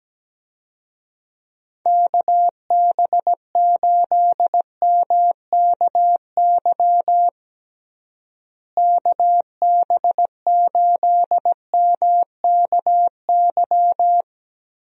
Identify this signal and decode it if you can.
Morse code, heard as a keyed tone, about 17 wpm, 700 Hz. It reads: KB8MKY KB8MKY
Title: USA Call Signs - 6 Characters - Repeated 3x - 17wpm